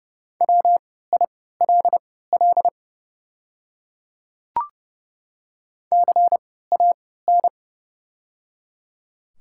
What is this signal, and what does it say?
Morse code, heard as a keyed tone, about 30 wpm, spaced as for 17 wpm, 700 Hz. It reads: WILL CAN
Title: 3x Character Spacing - Top 300 Words - 30wpm